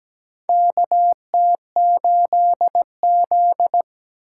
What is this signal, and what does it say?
Morse code, heard as a keyed tone, about 17 wpm, 700 Hz. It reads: KT8Z
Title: USA Call Signs - 3x Character Spacing - 17wpm